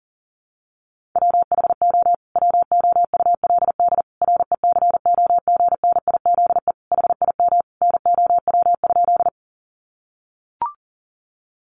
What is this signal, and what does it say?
Morse code, heard as a keyed tone, about 40 wpm, 700 Hz. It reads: WHO WOULD RECOGNIZE HIM NOW?
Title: Sentences from Top 1400 Words 40wpm